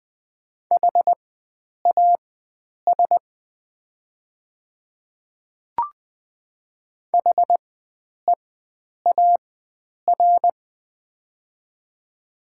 Morse code, heard as a keyed tone, about 20 wpm, spaced as for 9 wpm, 700 Hz. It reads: HAS HEAR
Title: 4x Character Spacing - Top 300 Words - 20wpm